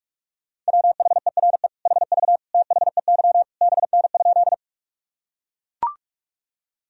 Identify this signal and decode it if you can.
Morse code, heard as a keyed tone, about 41 wpm, 700 Hz. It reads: WHERE HV THEY BN?
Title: Sentences from Top 200 Words and Abbreviations 45wpm